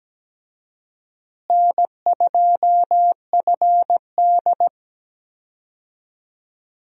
Morse code, heard as a keyed tone, about 17 wpm, 700 Hz. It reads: N2FD
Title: USA Call Signs - 4 Characters - Repeated 3x - 17wpm